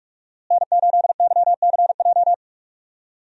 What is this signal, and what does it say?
Morse code, heard as a keyed tone, about 41 wpm, 700 Hz. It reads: N8YCJ
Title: USA Call Signs - 5 Characters - Repeated 3x - 45wpm